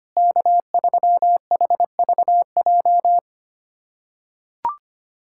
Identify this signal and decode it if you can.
Morse code, heard as a keyed tone, about 25 wpm, 700 Hz. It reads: K3HVJ
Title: USA Callsign DE Callsign - 5 Characters - 25wpm